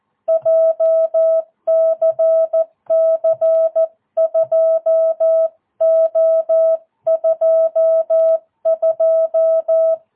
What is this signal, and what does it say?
Morse code, heard as a keyed tone, about 14 wpm, 650 Hz. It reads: JCC 2O22
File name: jh8cbh / 2022cw